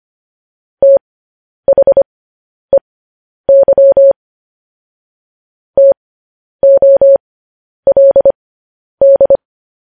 Morse code, heard as a keyed tone, about 25 wpm, spaced as for 10 wpm, 550 Hz. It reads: THEY TOLD